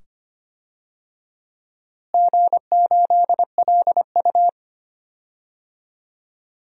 Morse code, heard as a keyed tone, about 25 wpm, 700 Hz. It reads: G8LU
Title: DXCC-100 Call Signs - 4 Characters - Repeated 2x - 25wpm